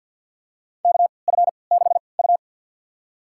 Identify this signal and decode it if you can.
Morse code, heard as a keyed tone, about 51 wpm, spaced as for 28 wpm, 700 Hz. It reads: KF6U
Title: USA Call Signs - 4 Characters - 3x Character Spacing - 50wpm